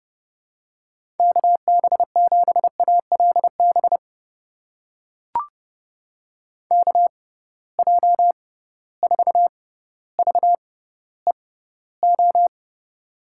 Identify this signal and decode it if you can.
Morse code, heard as a keyed tone, about 30 wpm, spaced as for 15 wpm, 700 Hz. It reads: KB7ALB KJ4VEO
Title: USA Call Signs - 6 Characters - 6x Character Spacing - 30wpm